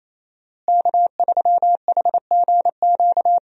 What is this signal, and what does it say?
Morse code, heard as a keyed tone, about 28 wpm, 700 Hz. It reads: K3HGQ
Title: USA Call Signs - 5 Characters - Repeated 3x - 28wpm